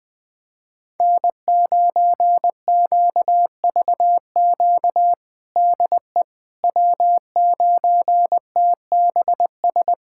Morse code, heard as a keyed tone, about 20 wpm, 700 Hz. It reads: N9QVQ DE W9TBS
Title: USA Callsign DE Callsign - 5 Characters - 20wpm